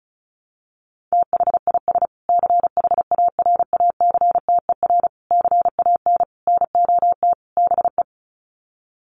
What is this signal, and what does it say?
Morse code, heard as a keyed tone, about 34 wpm, 700 Hz. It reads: THIS CHARACTER CAN NOT BE
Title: Sets of 5 Words Encore2 35wpm